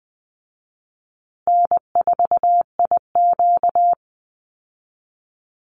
Morse code, heard as a keyed tone, about 19 wpm, 700 Hz. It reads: N4IQ
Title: Call Signs Repeated 2x - 20wpm